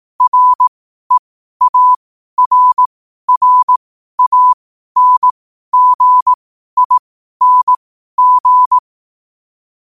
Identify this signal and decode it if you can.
Morse code, heard as a keyed tone, about 18 wpm, spaced as for 12 wpm, 1000 Hz. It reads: REARRANGING